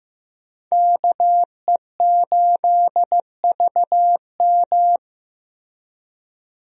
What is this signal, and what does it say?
Morse code, heard as a keyed tone, about 15 wpm, 700 Hz. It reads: KE8VM
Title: USA Call Signs - 5 Characters - Repeated 3x - 15wpm